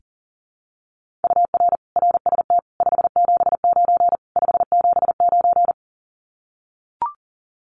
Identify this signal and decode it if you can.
Morse code, heard as a keyed tone, about 43 wpm, 700 Hz. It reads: UR RST 579 579
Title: Common QSO Phrases Encore 40wpm